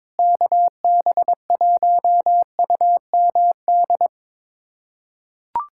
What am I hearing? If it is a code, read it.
Morse code, heard as a keyed tone, about 22 wpm, 700 Hz. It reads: KB1UMD
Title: USA Callsign DE Callsign - 6 Characters - 22wpm